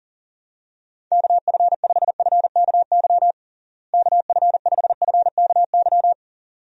Morse code, heard as a keyed tone, about 37 wpm, 700 Hz. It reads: KF5FKY KF5FKY
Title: USA Call Signs - 6 Characters - Repeated 2x - 40wpm